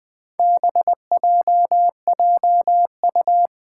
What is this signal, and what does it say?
Morse code, heard as a keyed tone, about 20 wpm, 700 Hz. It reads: BJJU
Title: Four Challenging Letter-Number Combinations 20wpm